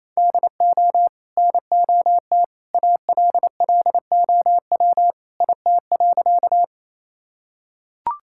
Morse code, heard as a keyed tone, about 27 wpm, 700 Hz. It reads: DO NOT ALLOW IT.
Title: Sentences from Top 1000 Words - Review Part 5 - 28wpm